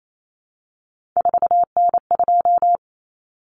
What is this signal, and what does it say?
Morse code, heard as a keyed tone, about 28 wpm, 700 Hz. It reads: 4N2
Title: Common Cut Numbers - 3-Digit 28wpm